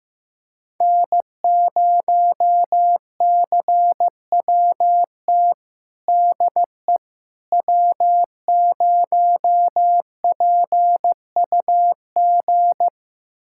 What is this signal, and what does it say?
Morse code, heard as a keyed tone, about 15 wpm, 700 Hz. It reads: N0CWT DE W0PUG
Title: USA Callsign DE Callsign - 5 Characters - 15wpm